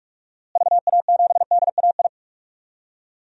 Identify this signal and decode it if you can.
Morse code, heard as a keyed tone, about 46 wpm, 700 Hz. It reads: VA7DAI
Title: DXCC-100 Call Signs - 6 Characters - Repeated 3x - 45wpm